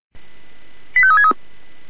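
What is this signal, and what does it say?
Five-tone selective call (ZVEI), heard as quick runs of five tones, tones of about 80 ms. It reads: 85363
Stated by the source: ZVEI five-tone selective call received via SDR.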